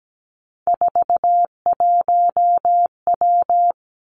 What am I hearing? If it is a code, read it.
Morse code, heard as a keyed tone, about 17 wpm, 700 Hz. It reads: 41W
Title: Three Challenging Letter-Number Combinations 17wpm